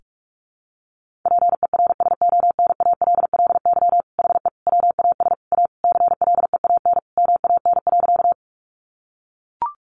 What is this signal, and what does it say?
Morse code, heard as a keyed tone, about 44 wpm, 700 Hz. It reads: PERSONALLY HE WAS A CLEAN MAN.
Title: Sentences from Top 1400 Words 45wpm